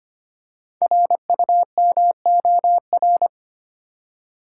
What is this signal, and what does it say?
Morse code, heard as a keyed tone, about 25 wpm, 700 Hz. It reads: RUMOR
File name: 5 Letter Words - First 3 Letters Given - 25wpm